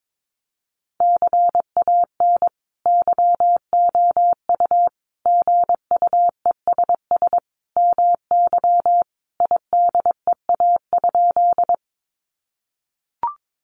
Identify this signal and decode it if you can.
Morse code, heard as a keyed tone, about 22 wpm, 700 Hz. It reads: CAN YOU GUESS MY IDEA?
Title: Sentences from Top 1500 Words 22wpm